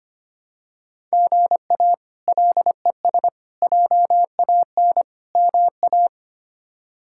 Morse code, heard as a keyed tone, about 25 wpm, 700 Hz. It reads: GA LES JAN MA
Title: K1USN SST QSO Practice - 25wpm